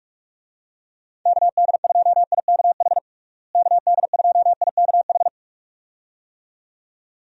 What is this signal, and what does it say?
Morse code, heard as a keyed tone, about 46 wpm, 700 Hz. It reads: KD2IKH KD2IKH
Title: USA Call Signs - 6 Characters - Repeated 2x - 45wpm